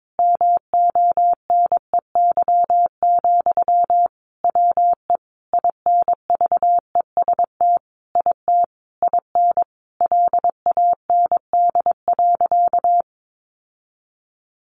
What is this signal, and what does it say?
Morse code, heard as a keyed tone, about 22 wpm, 700 Hz. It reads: MONEY, WE INVEST IT IN LAND.
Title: Sentences from Top 1100 Words 22wpm